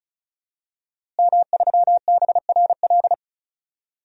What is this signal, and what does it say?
Morse code, heard as a keyed tone, about 34 wpm, 700 Hz. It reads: M3BRL
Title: DXCC-100 Call Signs - 5 Characters - Repeated 3x - 35wpm